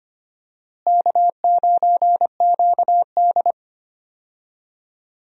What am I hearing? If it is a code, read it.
Morse code, heard as a keyed tone, about 25 wpm, 700 Hz. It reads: K9QD
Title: USA Call Signs - 4 Characters - Repeated 2x - 25wpm